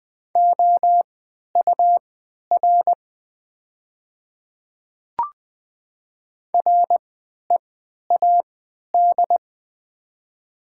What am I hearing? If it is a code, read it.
Morse code, heard as a keyed tone, about 20 wpm, spaced as for 11 wpm, 700 Hz. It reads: OUR READ
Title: 3x Character Spacing - Top 200 Words - 20wpm